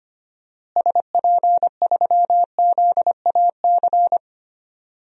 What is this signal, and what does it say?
Morse code, heard as a keyed tone, about 25 wpm, 700 Hz. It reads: SP3ZAC
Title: DXCC-100 Call Signs - 6 Characters - Repeated 3x - 25wpm